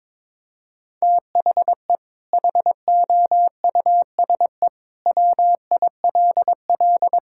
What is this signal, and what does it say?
Morse code, heard as a keyed tone, about 22 wpm, 700 Hz. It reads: THE HOUSE WILL
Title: Sentences from Top 1600 Words 22wpm